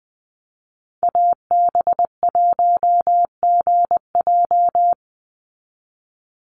Morse code, heard as a keyed tone, about 20 wpm, 700 Hz. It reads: AB1GJ
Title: Call Signs Repeated 3x - 20wpm